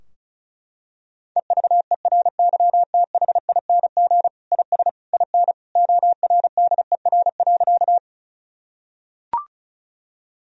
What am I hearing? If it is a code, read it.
Morse code, heard as a keyed tone, about 34 wpm, 700 Hz. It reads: EVERYTHING IS IN ORDER.
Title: Sentences from Top 1000 Words - Review Part 2 - 35wpm